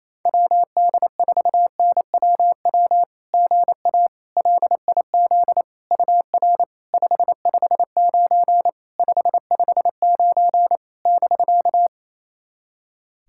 Morse code, heard as a keyed tone, about 28 wpm, 690 Hz. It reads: WD4NWW GA LIZ UR 559 559 <BK>
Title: SOTA QSO Practice 28wpm